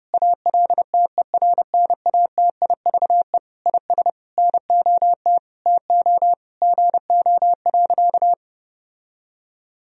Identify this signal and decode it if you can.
Morse code, heard as a keyed tone, about 32 wpm, 700 Hz. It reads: ALTERNATIVE IS NOT TO GO.
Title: Sentences from Top 1000 Words - Review Part 4 - 30wpm